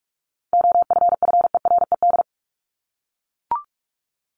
Morse code, heard as a keyed tone, about 44 wpm, 700 Hz. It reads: OFFERED
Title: ED Suffix - Top 100 Words with Base Words 45wpm